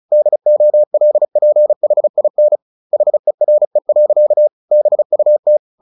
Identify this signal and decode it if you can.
Morse code, heard as a keyed tone, about 34 wpm, 600 Hz. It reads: DOLPHIN HERE. BUT
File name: the old man and the sea / the old man and the sea 1 35wpm